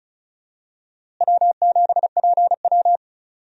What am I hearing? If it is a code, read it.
Morse code, heard as a keyed tone, about 34 wpm, 700 Hz. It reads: W7PW
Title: USA Call Signs - 4 Characters - Repeated 3x - 35wpm